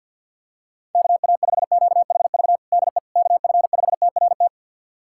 Morse code, heard as a keyed tone, about 49 wpm, 700 Hz. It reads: KA5QHV DE KF5TRT
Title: USA Callsign DE Callsign - 6 Characters - 50wpm